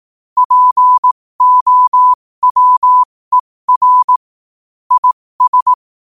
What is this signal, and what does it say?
Morse code, heard as a keyed tone, about 18 wpm, spaced as for 15 wpm, 1000 Hz. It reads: POWER IS
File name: qotdfiles / 15wpm